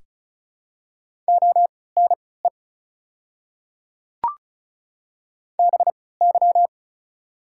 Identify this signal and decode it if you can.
Morse code, heard as a keyed tone, about 36 wpm, spaced as for 20 wpm, 700 Hz. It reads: ONE BY
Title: 3x Character Spacing - Top 100 Words - 35wpm